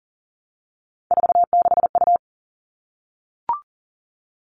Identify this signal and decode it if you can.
Morse code, heard as a keyed tone, about 40 wpm, 700 Hz. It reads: 46U
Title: All Cut Numbers - 3-Digit 40wpm